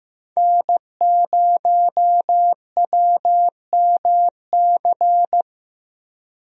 Morse code, heard as a keyed tone, about 15 wpm, 700 Hz. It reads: N0WMC
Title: USA Call Signs - 5 Characters - 15wpm